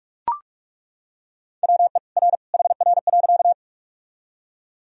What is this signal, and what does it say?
Morse code, heard as a keyed tone, about 42 wpm, 700 Hz. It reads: WE R HR.
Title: Sentences from Top 200 Words and Abbreviations 45wpm